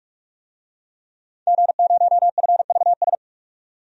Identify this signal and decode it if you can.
Morse code, heard as a keyed tone, about 47 wpm, 700 Hz. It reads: G0FVS
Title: DXCC-100 Call Signs - 5 Characters - 45wpm